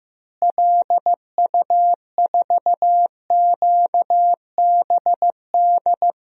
Morse code, heard as a keyed tone, about 15 wpm, 700 Hz. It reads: LU4QBD
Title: DXCC-100 Call Signs - 6 Characters - Repeated 3x - 15wpm